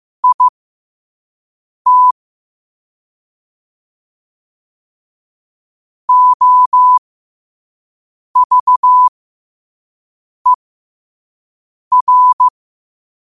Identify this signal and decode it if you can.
Morse code, heard as a keyed tone, about 15 wpm, spaced as for 5 wpm, 1000 Hz. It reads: IT OVER
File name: qotdfiles / 5wpm